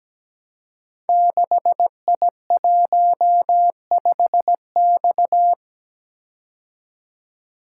Morse code, heard as a keyed tone, about 17 wpm, 700 Hz. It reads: 6I15X
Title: Code Groups - 5 Letter:Numbers - 17wpm